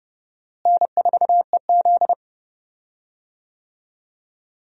Morse code, heard as a keyed tone, about 32 wpm, 700 Hz. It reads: N4EZ
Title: USA Call Signs - 4 Characters - Repeated 3x - 30wpm